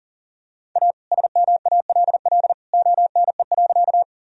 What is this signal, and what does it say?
Morse code, heard as a keyed tone, about 40 wpm, 700 Hz. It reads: A SMALL ONE.
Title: Sentences from Top 1000 Words - Review Part 1 - 40wpm